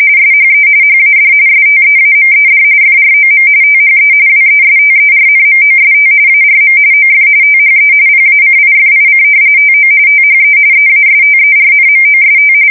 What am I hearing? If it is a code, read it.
RTTY, about 45.45 baud. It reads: SHE HAD STEAMED FROM THE WESTERNMOST ARM OF THE PACIFIC TO THE EASTERNMOST
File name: fsk170 / battleshipnj